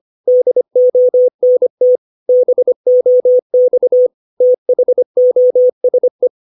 Morse code, heard as a keyed tone, about 25 wpm, 500 Hz. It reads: DONT BOX THOSE
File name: 632 words / 632 Words 25WPM